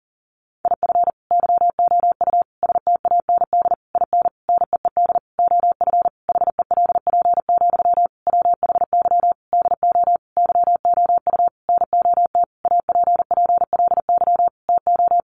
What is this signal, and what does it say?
Morse code, heard as a keyed tone, about 40 wpm, 700 Hz. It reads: IF YOU STAND IN NEED OF HELP, WHY DO YOU NOT APPLY TO
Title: Sentences from Top 1200 Words 40wpm